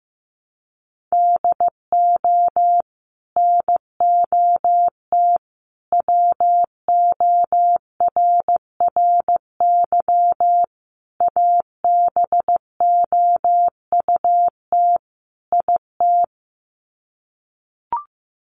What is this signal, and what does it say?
Morse code, heard as a keyed tone, about 15 wpm, 700 Hz. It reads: DO NOT WORRY ABOUT IT
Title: Sets of 5 Words 15wpm